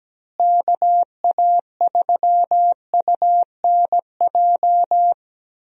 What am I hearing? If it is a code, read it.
Morse code, heard as a keyed tone, about 17 wpm, 700 Hz. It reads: KA3UNJ
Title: USA Call Signs - 6 Characters - Repeated 3x - 17wpm